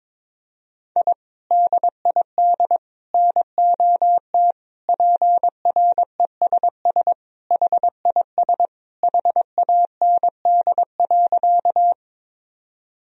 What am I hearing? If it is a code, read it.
Morse code, heard as a keyed tone, about 22 wpm, 700 Hz. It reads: I DID NOT PRESS HIS HAND.
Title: Sentences from Top 1000 Words - Review Part 1 - 22wpm